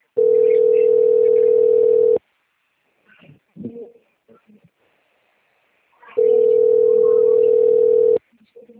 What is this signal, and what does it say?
Ringback tone (North America), 440 + 480 Hz.
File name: out- - - - - .50208